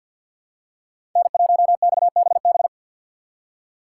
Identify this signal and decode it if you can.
Morse code, heard as a keyed tone, about 54 wpm, 700 Hz. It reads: N1XBB
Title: USA Call Signs - 5 Characters - 4x Character Spacing - 50wpm